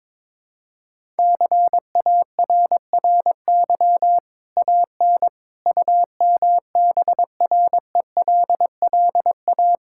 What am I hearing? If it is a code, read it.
Morse code, heard as a keyed tone, about 22 wpm, 700 Hz. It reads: CARRY AN UMBRELLA
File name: practice / 22wpm / Easy Sentences with Hints - 22wpm